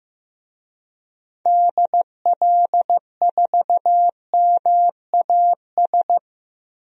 Morse code, heard as a keyed tone, about 15 wpm, 700 Hz. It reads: DL4MAS
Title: DXCC-100 Call Signs - 6 Characters - Repeated 3x - 15wpm